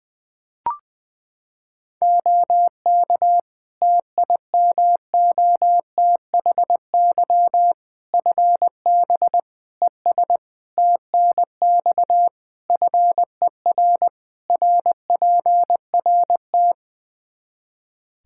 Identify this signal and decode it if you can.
Morse code, heard as a keyed tone, about 21 wpm, 700 Hz. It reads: OK TIMOTHY FB ES TNX FER RPRT
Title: Practice QSO - Exchange 2 - Info - 20wpm